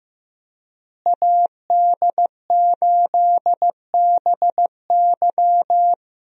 Morse code, heard as a keyed tone, about 15 wpm, 700 Hz. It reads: AD8BY
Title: USA Call Signs - 5 Characters - Repeated 2x - 15wpm